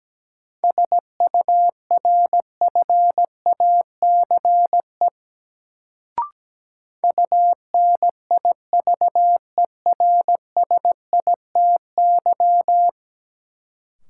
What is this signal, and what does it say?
Morse code, heard as a keyed tone, about 17 wpm, 700 Hz. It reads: SURFACE UNIVERSITY
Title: Sets of 4 Words - New Word Drill - 17wpm